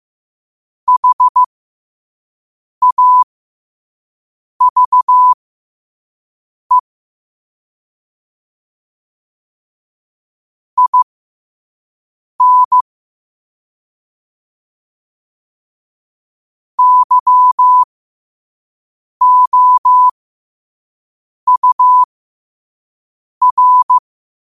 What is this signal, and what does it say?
Morse code, heard as a keyed tone, about 15 wpm, spaced as for 5 wpm, 1000 Hz. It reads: HAVE IN YOUR
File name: qotdfiles / 5wpm